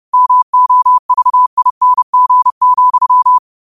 Morse code, heard as a keyed tone, about 30 wpm, 1000 Hz. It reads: MOVING,